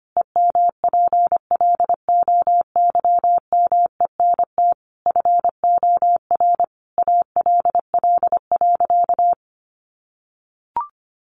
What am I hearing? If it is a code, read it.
Morse code, heard as a keyed tone, about 25 wpm, 700 Hz. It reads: EMPLOYMENT FOR ALL.
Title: Sentences from Top 1500 Words 25wpm